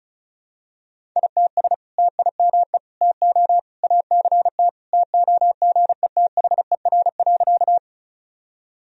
Morse code, heard as a keyed tone, about 36 wpm, 700 Hz. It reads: ITS TIME TO ACT TOGETHER.
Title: Sentences from Top 1000 Words - Review Part 1 - 35wpm